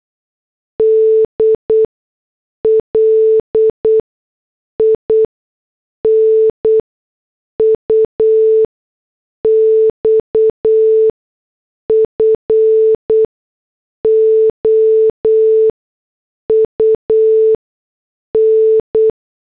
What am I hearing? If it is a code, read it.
Morse code, heard as a keyed tone, about 8 wpm, spaced as for 6 wpm, 440 Hz. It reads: D LINUX FOUN
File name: morse / headlines